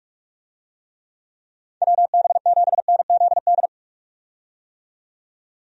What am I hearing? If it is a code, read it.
Morse code, heard as a keyed tone, about 45 wpm, 700 Hz. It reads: WB7NZD
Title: USA Call Signs - 6 Characters - 45wpm